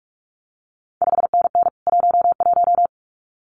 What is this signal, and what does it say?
Morse code, heard as a keyed tone, about 46 wpm, 700 Hz. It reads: 5NN 11
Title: CQ WW DX Contest Exchange Encore - 45wpm